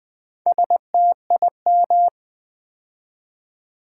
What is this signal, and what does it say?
Morse code, heard as a keyed tone, about 20 wpm, 700 Hz. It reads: STIM
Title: Build A Word - 2 Letters At A Time - 8-Letter Words - 20wpm